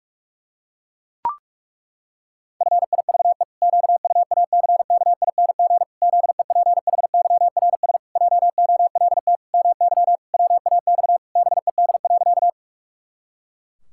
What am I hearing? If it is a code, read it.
Morse code, heard as a keyed tone, about 41 wpm, 700 Hz. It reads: FIVE QUACKING ZEPHYRS JOLT MY WAX BED.
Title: Pangram Challenge v2 - 45wpm